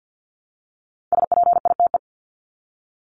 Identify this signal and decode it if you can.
Morse code, heard as a keyed tone, about 48 wpm, 700 Hz. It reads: SPITE
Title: Top 1751-2000 Words 50wpm